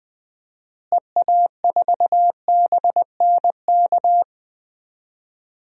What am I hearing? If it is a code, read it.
Morse code, heard as a keyed tone, about 20 wpm, 700 Hz. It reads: EA4BNK
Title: DXCC-100 Call Signs - 6 Characters - Repeated 2x - 20wpm